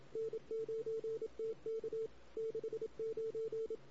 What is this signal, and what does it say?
Morse code, heard as a keyed tone, about 27 wpm, 440 Hz. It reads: N9TK 69
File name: CW / 2008-ss-cw-k5tr-op-n5rz -16:00